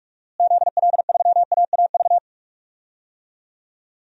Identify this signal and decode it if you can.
Morse code, heard as a keyed tone, about 46 wpm, 700 Hz. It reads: ZL3AAV
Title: DXCC-100 Call Signs - 6 Characters - Repeated 2x - 45wpm